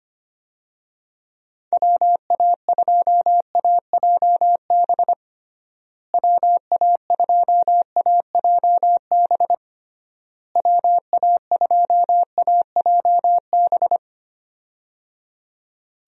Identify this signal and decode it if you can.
Morse code, heard as a keyed tone, about 25 wpm, 700 Hz. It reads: WA2AJB WA2AJB WA2AJB
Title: USA Call Signs - 6 Characters - Repeated 3x - 25wpm